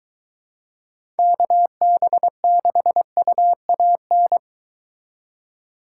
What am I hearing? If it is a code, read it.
Morse code, heard as a keyed tone, about 23 wpm, 700 Hz. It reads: KB6UAN